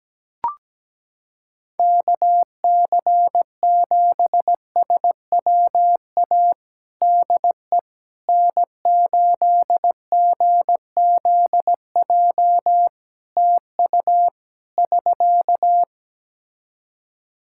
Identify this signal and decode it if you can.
Morse code, heard as a keyed tone, about 17 wpm, 700 Hz. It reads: KC7SWA DE N8GZJ TU <SK>
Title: Practice QSO - Exchange 4 - Ending - 17wpm